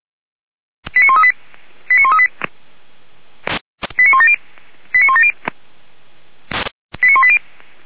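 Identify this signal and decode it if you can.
Five-tone selective call (ZVEI), heard as quick runs of five tones, tones of about 70 ms. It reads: 89138 89138 89170 89170 89180
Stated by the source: ZVEI five-tone selective call received on an SDR, from the Feuerwehr Orlamünde fire brigade server.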